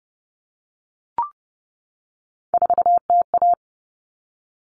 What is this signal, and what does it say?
Morse code, heard as a keyed tone, about 30 wpm, 700 Hz. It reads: 4TA
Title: All Cut Numbers - 3-Digit 30wpm